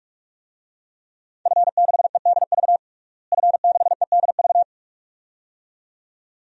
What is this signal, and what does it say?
Morse code, heard as a keyed tone, about 47 wpm, 700 Hz. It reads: F6EDV F6EDV
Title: DXCC-100 Call Signs - 5 Characters - Repeated 2x - 45wpm